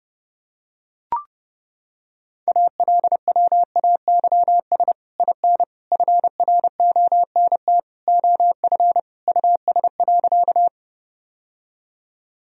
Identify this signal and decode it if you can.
Morse code, heard as a keyed tone, about 30 wpm, 700 Hz. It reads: ALWAYS IN FRONT OF US.
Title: Sentences from Top 1000 Words - Review Part 2 - 30wpm